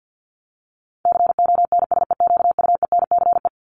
Morse code, heard as a keyed tone, about 50 wpm, 700 Hz. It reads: CONSEQUENCE
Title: LY Suffix - Top 100 Words with Base Words 50wpm